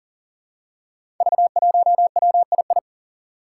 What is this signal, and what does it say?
Morse code, heard as a keyed tone, about 40 wpm, 700 Hz. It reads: V1WII
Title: Five Challenging Letter-Number Combinations 40wpm